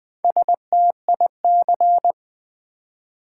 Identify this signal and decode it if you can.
Morse code, heard as a keyed tone, about 20 wpm, 700 Hz. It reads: STIC
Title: Build A Word - 1 Letter At A Time - 5-Letter Words - 20wpm